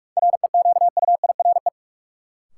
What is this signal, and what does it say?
Morse code, heard as a keyed tone, about 42 wpm, 700 Hz. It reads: REQUIRE
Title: Sentences from Top 900 Words - New Word Drill - 45wpm